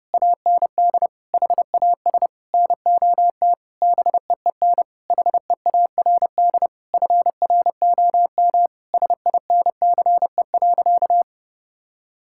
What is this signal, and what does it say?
Morse code, heard as a keyed tone, about 30 wpm, 700 Hz. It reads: AND HAS NOT BEEN HEARD FROM SINCE.
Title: Sentences from Top 1600 Words 30wpm